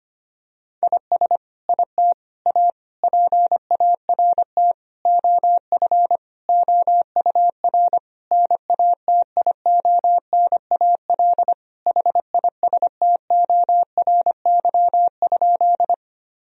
Morse code, heard as a keyed tone, about 25 wpm, 700 Hz. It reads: IS IT A PART OF OUR NATIONAL HISTORY?
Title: Sentences from Top 1000 Words - Review Part 4 - 25wpm